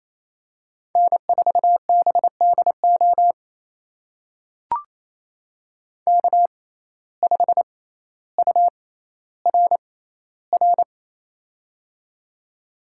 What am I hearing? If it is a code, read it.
Morse code, heard as a keyed tone, about 28 wpm, spaced as for 14 wpm, 700 Hz. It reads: N4BDO K5URR
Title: USA Call Signs - 5 Characters - 6x Character Spacing - 28wpm